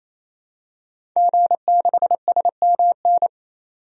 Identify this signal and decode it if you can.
Morse code, heard as a keyed tone, about 28 wpm, 700 Hz. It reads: G6SMN
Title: DXCC-100 Call Signs - 5 Characters - 28wpm